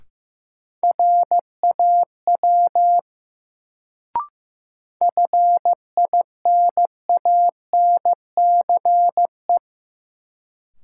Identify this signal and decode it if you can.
Morse code, heard as a keyed tone, about 15 wpm, 700 Hz. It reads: RAW FINANCE
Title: Sentences from Top 1500 Words - New Word Drill - 15wpm